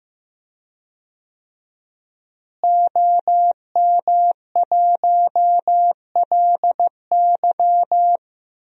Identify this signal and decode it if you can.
Morse code, heard as a keyed tone, about 15 wpm, 700 Hz. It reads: OM1LY
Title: DXCC-100 Call Signs - 5 Characters - Repeated 3x - 15wpm